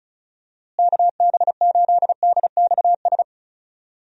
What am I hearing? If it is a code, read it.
Morse code, heard as a keyed tone, about 36 wpm, 700 Hz. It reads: KB8DXS
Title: USA Call Signs - 6 Characters - Repeated 2x - 35wpm